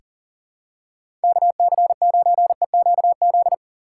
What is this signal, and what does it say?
Morse code, heard as a keyed tone, about 37 wpm, 700 Hz. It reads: KC9EQZ
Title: USA Call Signs - 6 Characters - 2x Character Spacing - 40wpm